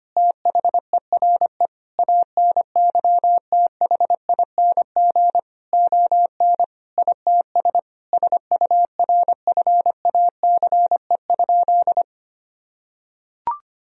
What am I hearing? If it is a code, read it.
Morse code, heard as a keyed tone, about 25 wpm, 700 Hz. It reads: THERE ANYTHING ON ITS SURFACE?
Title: Sentences from Top 1000 Words - Review Part 3 - 25wpm